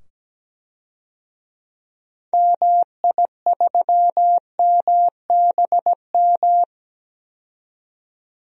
Morse code, heard as a keyed tone, about 17 wpm, 700 Hz. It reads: MI3MBM
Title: DXCC-100 Call Signs - 6 Characters - 17wpm